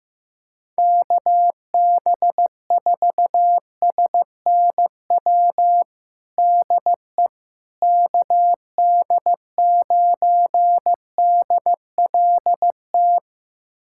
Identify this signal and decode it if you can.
Morse code, heard as a keyed tone, about 15 wpm, 700 Hz. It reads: KB4SNW DE KD9DLT
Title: USA Callsign DE Callsign 15wpm